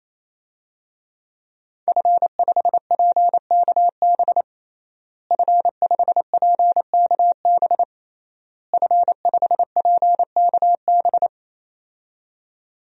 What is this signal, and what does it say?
Morse code, heard as a keyed tone, about 27 wpm, 700 Hz. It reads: F5PKB F5PKB F5PKB
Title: DXCC-100 Call Signs - 5 Characters - Repeated 3x - 28wpm